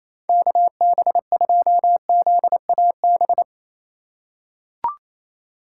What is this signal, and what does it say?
Morse code, heard as a keyed tone, about 28 wpm, 700 Hz. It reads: KB2ZAB
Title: USA Call Signs - 6x Character Spacing - 28wpm